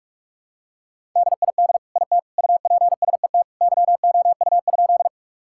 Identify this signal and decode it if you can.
Morse code, heard as a keyed tone, about 41 wpm, 700 Hz. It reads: DID IT UPSET YOU?
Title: Sentences from Top 1000 Words - Review Part 5 - 45wpm